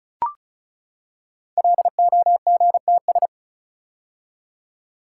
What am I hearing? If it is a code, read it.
Morse code, heard as a keyed tone, about 36 wpm, 700 Hz. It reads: LOGTS
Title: Code Groups - 5 Letter:Numbers - 35wpm